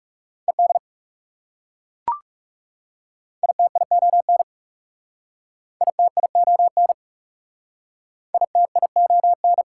Morse code, heard as a keyed tone, about 40 wpm, 700 Hz. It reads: ED ITION ITION ITION